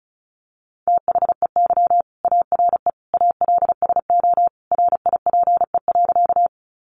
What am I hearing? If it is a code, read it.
Morse code, heard as a keyed tone, about 34 wpm, 700 Hz. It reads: THEY ARE ALSO RIPE.
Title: Additional Easy Phrases 35wpm